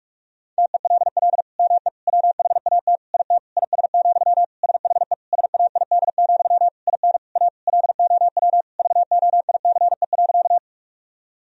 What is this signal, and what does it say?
Morse code, heard as a keyed tone, about 41 wpm, 700 Hz. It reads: TELL ME WHAT IT IS, SHE SAID, IN A LOW VOICE.
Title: Sentences from Top 1000 Words - Review Part 1 - 45wpm